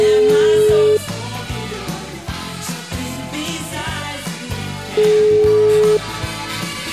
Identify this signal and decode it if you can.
Ringback tone (Europe, 425 Hz), with noise.